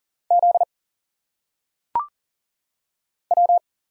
Morse code, heard as a keyed tone, about 40 wpm, 700 Hz. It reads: Z W
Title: Single Letters - Phonetic Pronunciation 40wpm